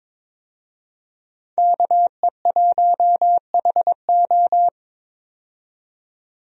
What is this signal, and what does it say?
Morse code, heard as a keyed tone, about 22 wpm, 700 Hz. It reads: KE1HO
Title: USA Call Signs - 5 Characters - 22wpm